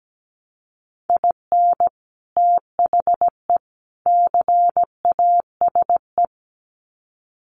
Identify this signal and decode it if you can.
Morse code, heard as a keyed tone, about 17 wpm, 700 Hz. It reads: IN THE CASE
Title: Sets of 3 Words 17wpm